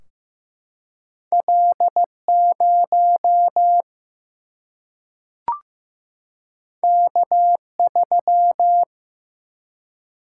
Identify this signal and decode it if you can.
Morse code, heard as a keyed tone, about 15 wpm, 700 Hz. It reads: L0 K3
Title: Single-Letter Single-Number 15wpm